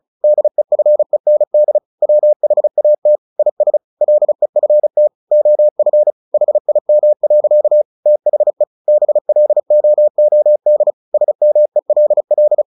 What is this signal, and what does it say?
Morse code, heard as a keyed tone, about 34 wpm, 600 Hz. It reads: DEFEND WHAT IS LEFT OF HIM. THE BLOOD SMELL
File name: the old man and the sea 2 35wpm